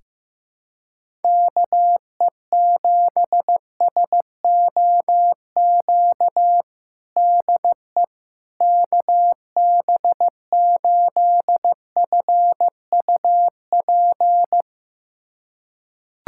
Morse code, heard as a keyed tone, about 16 wpm, 700 Hz. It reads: KE7SOQ DE KB8FUP
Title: USA Callsign DE Callsign 15wpm